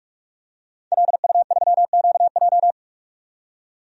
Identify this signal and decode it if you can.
Morse code, heard as a keyed tone, about 46 wpm, 700 Hz. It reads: LU3QJ
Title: DXCC-100 Call Signs - 5 Characters - 45wpm